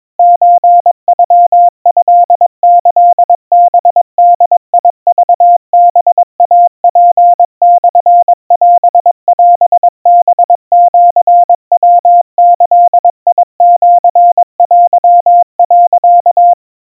Morse code (Japanese Wabun code), heard as a keyed tone, about 22 wpm, 700 Hz. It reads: ソノトキハボクハイツモオオハシヤギシテ、
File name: 80 cpm 700 Hz Crispy